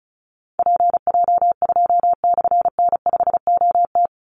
Morse code, heard as a keyed tone, about 34 wpm, 700 Hz. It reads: PJ2/N5OT
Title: Call Signs - 35wpm